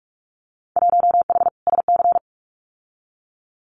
Morse code, heard as a keyed tone, about 46 wpm, 700 Hz. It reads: 1H SC
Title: Winter Field Day Exchanges 45wpm